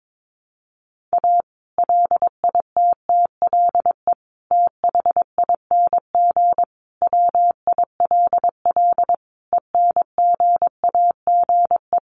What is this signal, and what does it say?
Morse code, heard as a keyed tone, about 22 wpm, 700 Hz. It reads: A LITTLE THING WILL ENGAGE
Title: Sentences from Top 1500 Words 22wpm